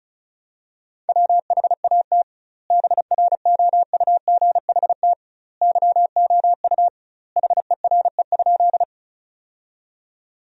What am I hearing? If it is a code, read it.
Morse code, heard as a keyed tone, about 34 wpm, 700 Hz. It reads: WHAT BROUGHT YOU HERE?
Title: 2x Word-Spacing - Sentences from Top 500 Words - 35wpm